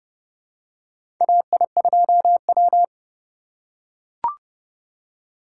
Morse code, heard as a keyed tone, about 29 wpm, 700 Hz. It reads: AI2W
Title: USA Call Signs - 4 Characters - 4x Character Spacing - 30wpm